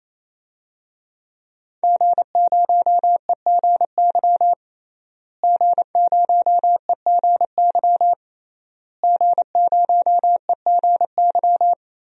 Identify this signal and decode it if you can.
Morse code, heard as a keyed tone, about 28 wpm, 700 Hz. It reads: G0EGY G0EGY G0EGY
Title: DXCC-100 Call Signs - 5 Characters - Repeated 3x - 28wpm